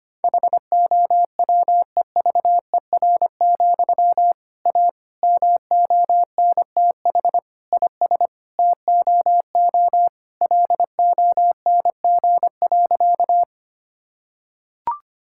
Morse code, heard as a keyed tone, about 25 wpm, 700 Hz. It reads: HOWEVER, A MONTH IS TOO LONG.
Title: Sentences from Top 1000 Words - Review Part 5 - 25wpm